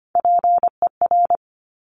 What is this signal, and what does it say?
Morse code, heard as a keyed tone, about 25 wpm, 700 Hz. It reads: PER
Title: Top Three Letter Combinations in Words 25wpm